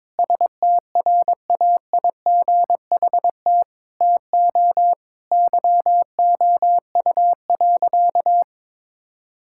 Morse code, heard as a keyed tone, about 22 wpm, 700 Hz. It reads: STRAIGHT TO YOU.
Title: Sentences from Top 1000 Words - Review Part 6 - 22wpm